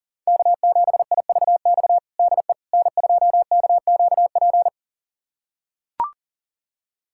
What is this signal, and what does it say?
Morse code, heard as a keyed tone, about 40 wpm, 700 Hz. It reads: K7IVX DE N2KQP
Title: USA Callsign DE Callsign - 5 Characters - 40wpm